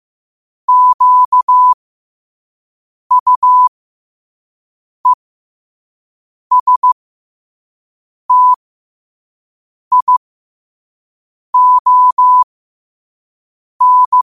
Morse code, heard as a keyed tone, about 15 wpm, spaced as for 5 wpm, 1000 Hz. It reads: QUESTION
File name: qotdfiles / 5wpm